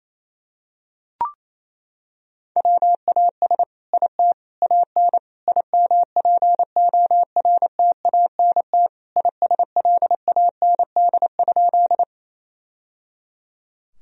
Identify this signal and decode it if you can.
Morse code, heard as a keyed tone, about 27 wpm, 690 Hz. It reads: WAS IT AN IMPORTANT ISLAND?
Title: Sentences from Top 1000 Words - Review Part 3 - 28wpm